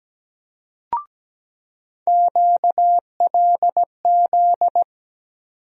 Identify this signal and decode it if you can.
Morse code, heard as a keyed tone, about 17 wpm, 700 Hz. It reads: QLZ
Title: USA Call Sign Postfixes - 3 Letters 17wpm